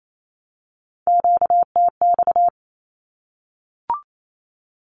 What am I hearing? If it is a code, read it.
Morse code, heard as a keyed tone, about 28 wpm, 700 Hz. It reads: QTX
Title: Q-Codes 28wpm